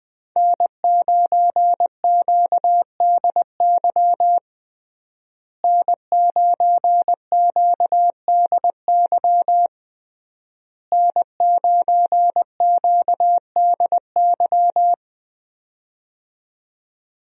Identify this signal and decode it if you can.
Morse code, heard as a keyed tone, about 20 wpm, 700 Hz. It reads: N9QDY N9QDY N9QDY
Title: USA Call Signs - 5 Characters - Repeated 3x - 20wpm